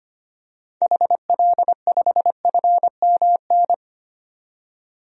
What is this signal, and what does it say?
Morse code, heard as a keyed tone, about 25 wpm, 700 Hz. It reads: HL5FMN
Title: DXCC-100 Call Signs - 6 Characters - 25wpm